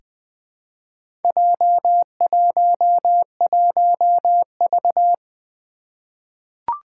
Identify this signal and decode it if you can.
Morse code, heard as a keyed tone, about 20 wpm, 700 Hz. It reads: J11V
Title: Four Challenging Letter-Number Combinations 20wpm